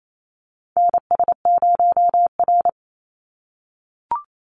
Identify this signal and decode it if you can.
Morse code, heard as a keyed tone, about 28 wpm, 700 Hz. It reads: NS0R
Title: Call Signs - 28wpm